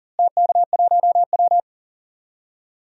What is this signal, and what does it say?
Morse code, heard as a keyed tone, about 37 wpm, 700 Hz. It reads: TK1W
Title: DXCC-100 Call Signs - 4 Characters - Repeated 2x - 40wpm